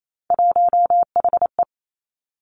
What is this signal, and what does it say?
Morse code, heard as a keyed tone, about 28 wpm, 690 Hz. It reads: 1HE
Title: Three Challenging Letter-Number Combinations 28wpm